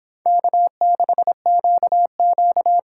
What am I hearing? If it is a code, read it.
Morse code, heard as a keyed tone, about 25 wpm, 700 Hz. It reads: K6QQ